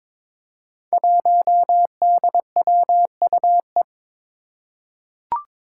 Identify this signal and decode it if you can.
Morse code, heard as a keyed tone, about 22 wpm, 700 Hz. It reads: 1DWUE
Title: Five Challenging Letter-Number Combinations 22wpm